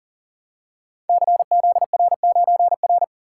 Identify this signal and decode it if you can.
Morse code, heard as a keyed tone, about 40 wpm, 700 Hz. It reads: CZR9R
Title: Code Groups - 5 Letter:Numbers - 40wpm